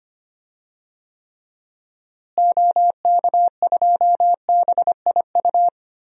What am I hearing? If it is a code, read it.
Morse code, heard as a keyed tone, about 25 wpm, 700 Hz. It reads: OK2BIU
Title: DXCC-100 Call Signs - 6 Characters - Repeated 2x - 25wpm